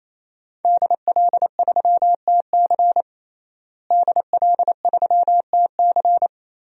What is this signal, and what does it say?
Morse code, heard as a keyed tone, about 27 wpm, 700 Hz. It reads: DL3TC DL3TC
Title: DXCC-100 Call Signs - 5 Characters - Repeated 3x - 28wpm